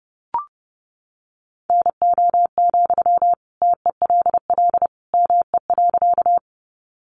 Morse code, heard as a keyed tone, about 30 wpm, 700 Hz. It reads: NO, TELL ME.
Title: Sentences from Top 200 Words 30wpm